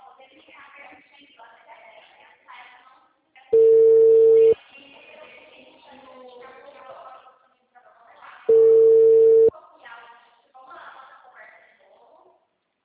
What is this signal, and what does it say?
Ringback tone (China, 450 Hz), with speech.
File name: out- -2016- - - .28800